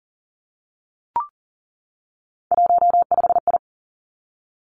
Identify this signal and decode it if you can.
Morse code, heard as a keyed tone, about 42 wpm, 700 Hz. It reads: 15I
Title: Three Challenging Letter-Number Combinations 40wpm